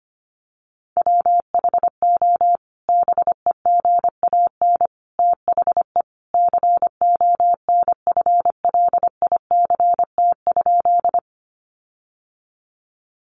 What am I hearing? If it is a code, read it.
Morse code, heard as a keyed tone, about 25 wpm, 700 Hz. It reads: WHO BEGAN THE CONFLICT?
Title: Sentences from Top 1300 Words 25wpm